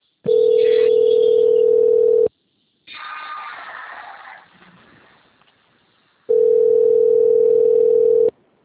Ringback tone (North America), 440 + 480 Hz.